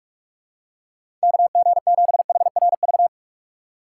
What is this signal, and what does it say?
Morse code, heard as a keyed tone, about 47 wpm, 700 Hz. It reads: KG7HRV
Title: USA Call Signs - 6 Characters - 4x Character Spacing - 45wpm